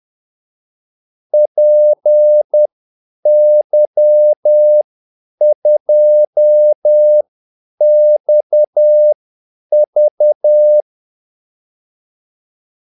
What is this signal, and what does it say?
Morse code, heard as a keyed tone, about 10 wpm, spaced as for 8 wpm, 600 Hz. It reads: PY2XV
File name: Callsigns-10-8wpm